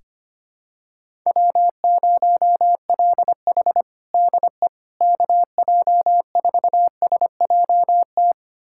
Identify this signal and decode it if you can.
Morse code, heard as a keyed tone, about 25 wpm, 700 Hz. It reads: W0LH DE KJ4SJT
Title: USA Callsign DE Callsign 25wpm